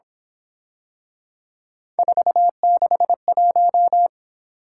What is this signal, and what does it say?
Morse code, heard as a keyed tone, about 26 wpm, 700 Hz. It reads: 461
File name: Speed-Racing - Three Letter-Number Combinations - 22wpm